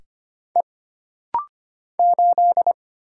Morse code, heard as a keyed tone, about 25 wpm, 700 Hz. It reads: E 8
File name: Single Character (Letter:Number:Punctuation:Prosign) - Rapid-Fire - 25wpm